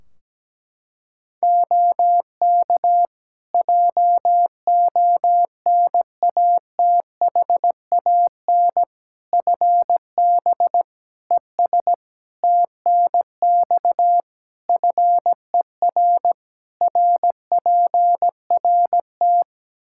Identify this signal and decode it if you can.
Morse code, heard as a keyed tone, about 17 wpm, 700 Hz. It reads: OK JONATHAN FB ES TNX FER RPRT
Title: Practice QSO - Exchange 2 - Info - 17wpm